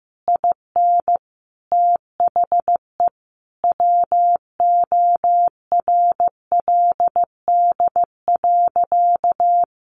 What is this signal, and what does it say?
Morse code, heard as a keyed tone, about 15 wpm, 700 Hz. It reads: IN THE WORLD.
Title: Sentences from Top 700 Words 15wpm